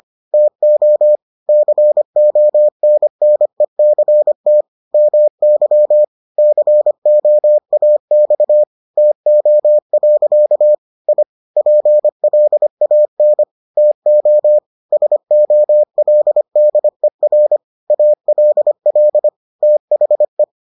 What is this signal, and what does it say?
Morse code, heard as a keyed tone, about 25 wpm, 600 Hz. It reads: TO CONNECT MY COAX TO. I PLAN TO SOLDER ALL THE